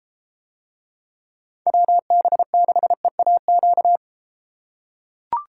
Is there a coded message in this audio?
Yes: Morse code from 1.7 to 4.0 s, heard as a keyed tone, about 33 wpm, 700 Hz. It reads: WB6EAQ